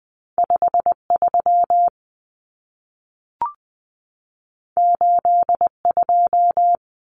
Morse code, heard as a keyed tone, about 20 wpm, 700 Hz. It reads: 53 82
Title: Common Cut Numbers - 2-Digit 20wpm